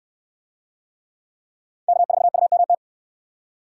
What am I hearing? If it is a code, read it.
Morse code, heard as a keyed tone, about 67 wpm, 700 Hz. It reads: K4UMT